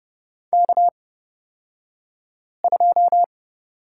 Morse code, heard as a keyed tone, about 30 wpm, spaced as for 5 wpm, 700 Hz. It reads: K2
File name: Lesson 37d - Letter x - Two Characters 30wpm